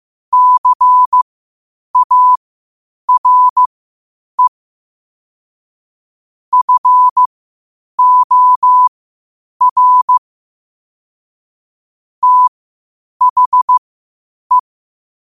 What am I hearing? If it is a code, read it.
Morse code, heard as a keyed tone, about 15 wpm, spaced as for 8 wpm, 1000 Hz. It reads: CARE FOR THE
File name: qotdfiles / 8wpm